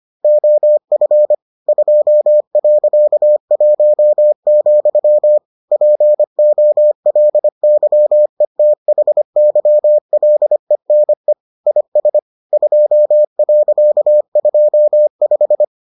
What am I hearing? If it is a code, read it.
Morse code, heard as a keyed tone, about 25 wpm, 600 Hz. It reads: OF 2.1, POLYETHYLENE IS 2.25